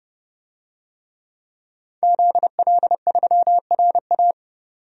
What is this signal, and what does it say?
Morse code, heard as a keyed tone, about 31 wpm, 700 Hz. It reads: ZL3RA
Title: DXCC-100 Call Signs - 5 Characters - 30wpm